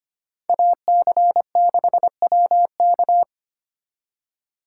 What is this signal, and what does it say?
Morse code, heard as a keyed tone, about 31 wpm, 700 Hz. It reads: AC6WK